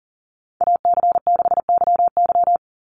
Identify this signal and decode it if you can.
Morse code, heard as a keyed tone, about 40 wpm, 700 Hz. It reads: AC6YY
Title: Call Signs Repeated 2x - 40wpm